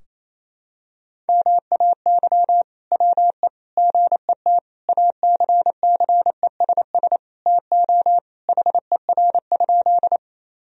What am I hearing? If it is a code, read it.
Morse code, heard as a keyed tone, about 28 wpm, 700 Hz. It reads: MAY WE GET ACCESS TO HER?
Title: Sentences from Top 1000 Words - Review Part 4 - 28wpm